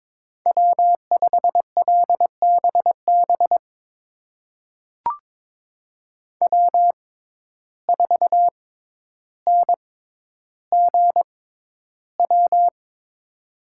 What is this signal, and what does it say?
Morse code, heard as a keyed tone, about 22 wpm, spaced as for 11 wpm, 700 Hz. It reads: W5LBB W4NGW
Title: USA Call Signs - 5 Characters - 6x Character Spacing - 22wpm